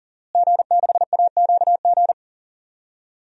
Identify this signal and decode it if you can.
Morse code, heard as a keyed tone, about 40 wpm, 700 Hz. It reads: G6AQG
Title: DXCC-100 Call Signs - 5 Characters - Repeated 3x - 40wpm